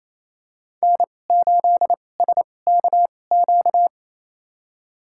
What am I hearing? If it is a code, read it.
Morse code, heard as a keyed tone, about 28 wpm, spaced as for 20 wpm, 700 Hz. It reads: N8SKQ
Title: USA Call Signs - 2x Character Spacing - 28wpm